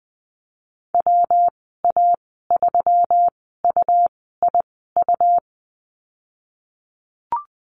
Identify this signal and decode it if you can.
Morse code, heard as a keyed tone, about 20 wpm, spaced as for 15 wpm, 700 Hz. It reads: WA3UIU
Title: 154 Lesson 25g - Letter p - Call Signs 20wpm